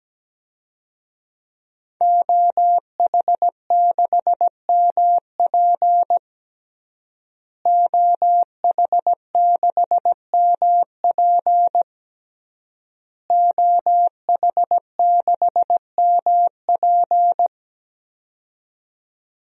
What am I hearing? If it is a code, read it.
Morse code, heard as a keyed tone, about 17 wpm, 700 Hz. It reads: OH6MP OH6MP OH6MP
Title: DXCC-100 Call Signs - 5 Characters - Repeated 3x - 17wpm